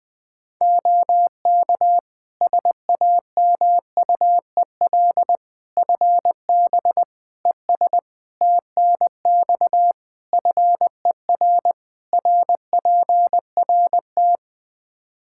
Morse code, heard as a keyed tone, about 20 wpm, 700 Hz. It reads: OK SAMUEL FB ES TNX FER RPRT
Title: K1USN SST QSO Practice 20wpm